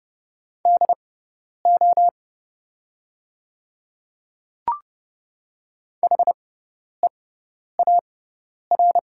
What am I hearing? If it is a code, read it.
Morse code, heard as a keyed tone, about 30 wpm, spaced as for 10 wpm, 700 Hz. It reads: DO HEAR
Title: 6x Character Spacing - Top 300 Words - 30wpm